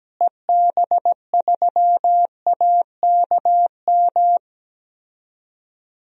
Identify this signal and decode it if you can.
Morse code, heard as a keyed tone, about 17 wpm, 700 Hz. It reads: EB3AKM
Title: DXCC-100 Call Signs - 6 Characters - Repeated 3x - 17wpm